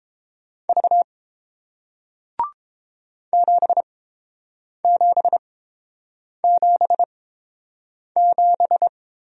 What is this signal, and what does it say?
Morse code, heard as a keyed tone, about 30 wpm, spaced as for 7 wpm, 700 Hz. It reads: V 7777